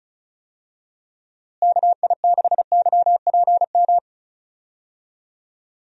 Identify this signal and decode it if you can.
Morse code, heard as a keyed tone, about 36 wpm, 700 Hz. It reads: KI6YPM
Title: USA Call Signs - 6 Characters - Repeated 3x - 35wpm